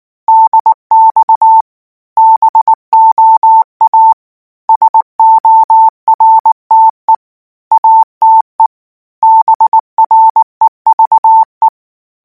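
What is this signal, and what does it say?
Morse code, heard as a keyed tone, about 19 wpm, 890 Hz. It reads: DX BOA SORTE ATE BREVE
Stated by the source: Morse code practice text at 18 wpm, from Portal do CW.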